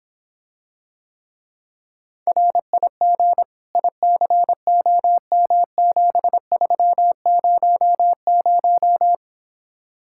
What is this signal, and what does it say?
Morse code, heard as a keyed tone, about 26 wpm, 700 Hz. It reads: RIG ICOM7300